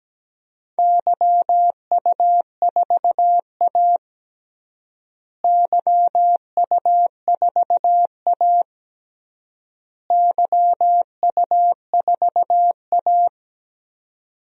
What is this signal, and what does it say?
Morse code, heard as a keyed tone, about 17 wpm, 700 Hz. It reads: YU4A YU4A YU4A
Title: DXCC-100 Call Signs - 4 Characters - Repeated 3x - 17wpm